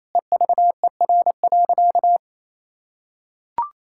Morse code, heard as a keyed tone, about 28 wpm, 700 Hz. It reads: EVER.
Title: Sentences from Top 1000 Words - Review Part 6 - 28wpm